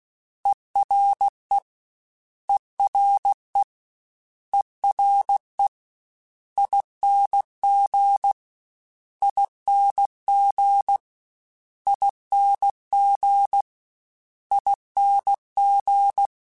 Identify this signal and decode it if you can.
Morse code, heard as a keyed tone, about 16 wpm, 800 Hz. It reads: ERE ERE ERE ING ING ING ING
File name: Special INGs / Special INGs-mp3s / Ere ing @15